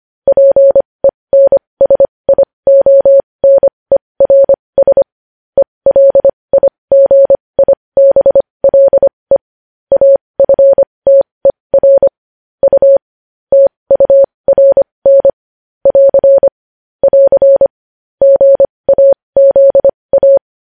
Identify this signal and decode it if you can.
Morse code, heard as a keyed tone, about 25 wpm, spaced as for 20 wpm, 550 Hz. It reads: PENSIONERS ELIGIBLE AFTER U TURN <AR> <AR> GAZA
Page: news headlines in morse code at 20 wpm